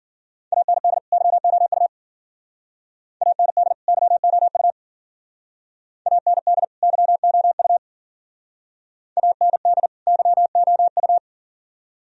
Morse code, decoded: AND YOU AND YOU AND YOU AND YOU

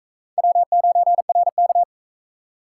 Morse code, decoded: W9RK

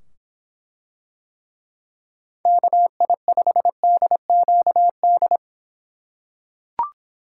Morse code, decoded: KI5DQD

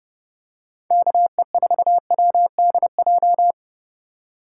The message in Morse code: KE4WDJ

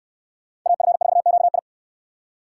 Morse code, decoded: AF4QI

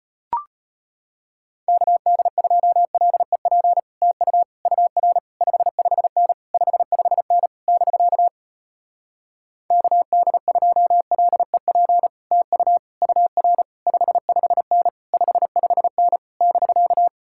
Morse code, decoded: KD2LEP TU UR 55N 55N <BK> KD2LEP TU UR 55N 55N <BK>

